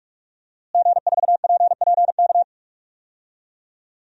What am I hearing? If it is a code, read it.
Morse code, heard as a keyed tone, about 46 wpm, 700 Hz. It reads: G4PPK
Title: DXCC-100 Call Signs - 5 Characters - Repeated 2x - 45wpm